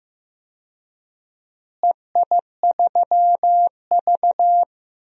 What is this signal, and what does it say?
Morse code, heard as a keyed tone, about 15 wpm, 700 Hz. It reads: EI3V
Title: DXCC-100 Call Signs - 4 Characters - Repeated 3x - 15wpm